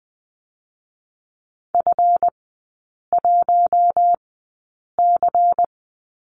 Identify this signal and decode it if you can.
Morse code, heard as a keyed tone, about 20 wpm, spaced as for 8 wpm, 700 Hz. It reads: F1C
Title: 138 Lesson 23e - Letter f - Three Characters 20wpm